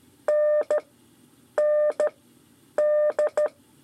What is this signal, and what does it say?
Morse code, heard as a keyed tone, about 14 wpm, spaced as for 8 wpm, 600 Hz. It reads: N ND
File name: 5 LEZIONE RIPASSO 2